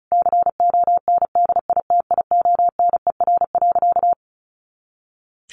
Morse code, heard as a keyed tone, about 34 wpm, 700 Hz. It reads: CONDITIONER.
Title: Easy Phrases 35wpm